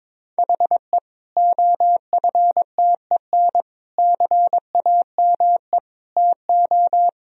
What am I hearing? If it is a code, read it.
Morse code, heard as a keyed tone, about 22 wpm, 700 Hz. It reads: HE OFTEN CAME TO
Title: Sentences from Top 1000 Words - Review Part 1 - 22wpm